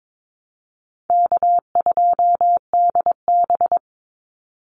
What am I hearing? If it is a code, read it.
Morse code, heard as a keyed tone, about 22 wpm, 700 Hz. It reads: K2DB
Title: Easy Call Signs - 22wpm